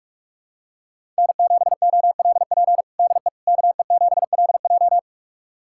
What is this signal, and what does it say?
Morse code, heard as a keyed tone, about 46 wpm, 700 Hz. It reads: N7OLP DE KE7LJ
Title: USA Callsign DE Callsign - 5 Characters - 45wpm